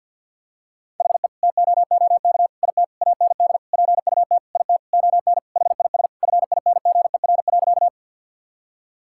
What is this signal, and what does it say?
Morse code, heard as a keyed tone, about 50 wpm, 700 Hz. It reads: HE TOOK IT AND PUT IT ON HIS FINGER.
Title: Sentences from Top 1600 Words 50wpm